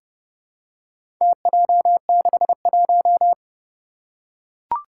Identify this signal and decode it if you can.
Morse code, heard as a keyed tone, about 30 wpm, 700 Hz. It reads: TJ61
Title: Four Challenging Letter-Number Combinations 30wpm